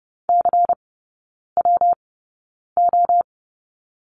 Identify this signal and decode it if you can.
Morse code, heard as a keyed tone, about 30 wpm, spaced as for 9 wpm, 700 Hz. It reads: CWO
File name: Lesson 19e - Letter w - Three Characters 30wpm